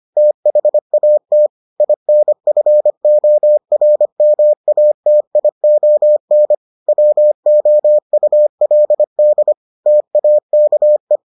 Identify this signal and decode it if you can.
Morse code, heard as a keyed tone, about 25 wpm, 600 Hz. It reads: THAT INFORMATION WOULD TAKE